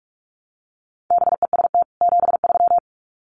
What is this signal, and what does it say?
Morse code, heard as a keyed tone, about 44 wpm, 700 Hz. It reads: BEST 73
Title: Common QSO Phrases 45wpm